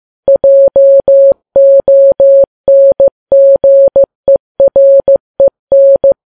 Morse code, heard as a keyed tone, about 15 wpm, 550 Hz. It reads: JONGEREN
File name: Files Web tls / Morse on9fv / morse15kps